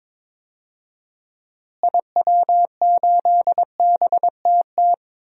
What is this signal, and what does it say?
Morse code, heard as a keyed tone, about 22 wpm, 700 Hz. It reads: IW8BTT
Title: DXCC-100 Call Signs - 6 Characters - Repeated 3x - 22wpm